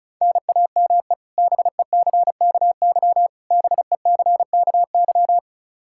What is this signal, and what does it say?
Morse code, heard as a keyed tone, about 34 wpm, 700 Hz. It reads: NAME BECKY BECKY
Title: K1USN SST QSO Practice 35wpm